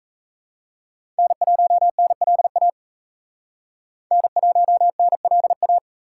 Morse code, heard as a keyed tone, about 36 wpm, 700 Hz. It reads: N1NLA N1NLA